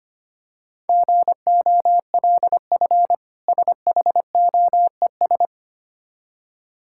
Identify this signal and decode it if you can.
Morse code, heard as a keyed tone, about 26 wpm, 700 Hz. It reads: GOLF SHOES
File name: practice / 25wpm / Difficult Words - Speed-Racing with Hints - 25wpm